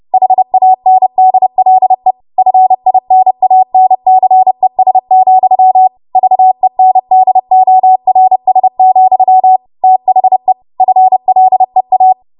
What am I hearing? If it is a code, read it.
Morse code, heard as a keyed tone, about 30 wpm, 750 Hz. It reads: HANDLE FINANCES, VENDORS, THE FLEA
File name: Morse / 30WPM